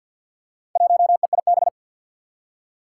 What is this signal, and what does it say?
Morse code, heard as a keyed tone, about 50 wpm, 700 Hz. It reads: 1EIB
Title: Four Challenging Letter-Number Combinations 50wpm